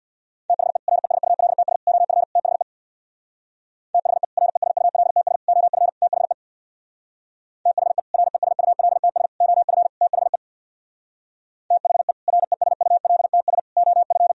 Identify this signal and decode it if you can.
Morse code, heard as a keyed tone, about 64 wpm, 700 Hz. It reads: THE RESULTS OF THE THE RESULTS OF THE THE RESULTS OF THE THE RESULTS OF